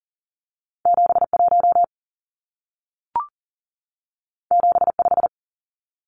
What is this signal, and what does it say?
Morse code, heard as a keyed tone, about 40 wpm, 700 Hz. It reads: 71 75
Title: Numbers - 2-Digit 40wpm